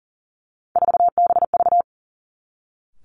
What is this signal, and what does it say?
Morse code, heard as a keyed tone, about 39 wpm, 700 Hz. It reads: 4BV
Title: Three Challenging Letter-Number Combinations 40wpm